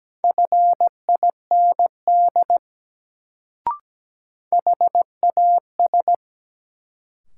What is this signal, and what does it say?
Morse code, heard as a keyed tone, about 17 wpm, 700 Hz. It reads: FIND HAS
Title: Sentences from Top 100 Words - New Word Drill - 17wpm